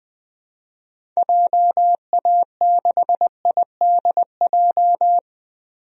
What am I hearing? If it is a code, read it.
Morse code, heard as a keyed tone, about 21 wpm, 700 Hz. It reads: JA6IDJ
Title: DXCC-100 Call Signs - 6 Characters - 20wpm